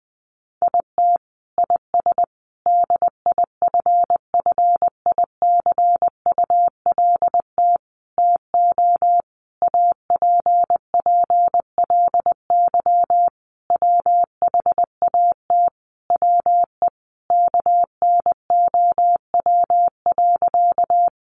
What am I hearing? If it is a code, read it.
Morse code, heard as a keyed tone, about 20 wpm, 700 Hz. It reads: IT IS DIFFICULT TO APPLY WHAT WE KNOW.
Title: Sentences from Top 1200 Words 20wpm